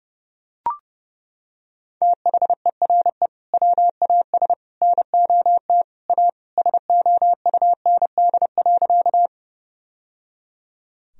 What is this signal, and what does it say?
Morse code, heard as a keyed tone, about 30 wpm, 700 Hz. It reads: THERE WAS NOT A SOUND.
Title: Sentences from Top 1000 Words - Review Part 1 - 30wpm